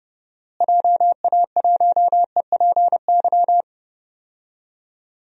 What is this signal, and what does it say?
Morse code, heard as a keyed tone, about 30 wpm, 700 Hz. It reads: JA1EPY
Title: DXCC-100 Call Signs - 6 Characters - Repeated 2x - 30wpm